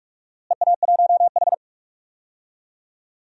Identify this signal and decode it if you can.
Morse code, heard as a keyed tone, about 43 wpm, 700 Hz. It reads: EA1H
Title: DXCC-100 Call Signs - 4 Characters - Repeated 3x - 45wpm